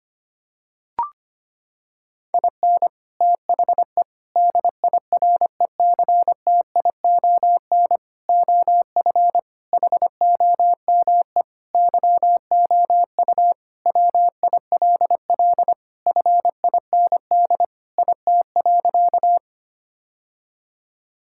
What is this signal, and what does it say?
Morse code, heard as a keyed tone, about 25 wpm, 700 Hz. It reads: IN THE DIRECTION OF HOME YOU WILL FIND IT.
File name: practice / 25wpm / Sentences with 4-Word Phrases - 25wpm